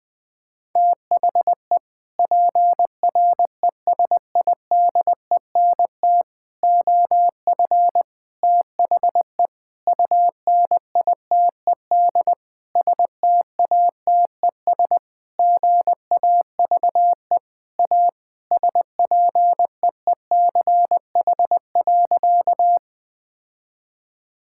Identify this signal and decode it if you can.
Morse code, heard as a keyed tone, about 20 wpm, 700 Hz. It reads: THE PRESIDENT OF THE UNITED STATES GAVE A SPEECH.
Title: Sentences with 5-Word Phrases - Spoken Beginning - 20wpm